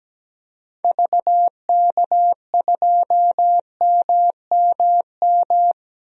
Morse code, heard as a keyed tone, about 17 wpm, 700 Hz. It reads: VK2MMM
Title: DXCC-100 Call Signs - 6 Characters - 17wpm